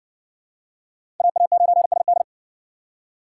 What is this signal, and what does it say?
Morse code, heard as a keyed tone, about 52 wpm, 700 Hz. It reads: AA9SD